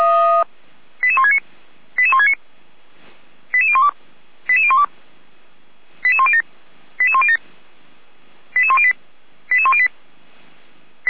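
Five-tone selective call (ZVEI), heard as quick runs of five tones, tones of about 70 ms. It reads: 88169 88169 88012 88012 88107 88107 88108 88108
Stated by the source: ZVEI five-tone selective call 88251 received by SDR.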